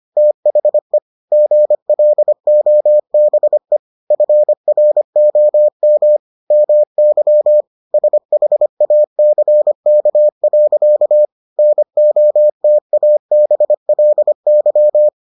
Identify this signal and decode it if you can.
Morse code, heard as a keyed tone, about 25 wpm, 600 Hz. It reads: THE GLOBE FROM MY SHACK. NOTABLY